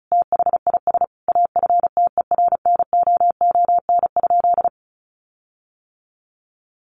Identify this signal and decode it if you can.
Morse code, heard as a keyed tone, about 34 wpm, 700 Hz. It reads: THIS AFTERNOON?
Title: Sentences from Top 1500 Words 35wpm